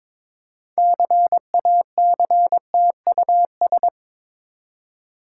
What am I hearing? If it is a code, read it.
Morse code, heard as a keyed tone, about 22 wpm, 690 Hz. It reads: CACTUS
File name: Difficult Words with Hints - 22wpm